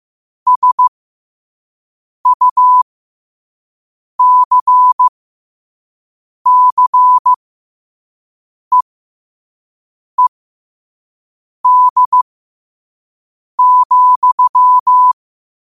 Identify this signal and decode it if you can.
Morse code, heard as a keyed tone, about 15 wpm, spaced as for 5 wpm, 1000 Hz. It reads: SUCCEED,
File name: qotdfiles / 5wpm